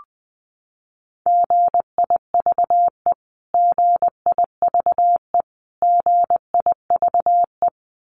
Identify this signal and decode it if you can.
Morse code, heard as a keyed tone, about 19 wpm, 700 Hz. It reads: GIVE GIVE GIVE
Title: Top 500 Words Repeated 20wpm